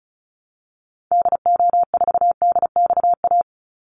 Morse code, heard as a keyed tone, about 34 wpm, 700 Hz. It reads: DO4DXA
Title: Call Signs Repeated 3x - 35wpm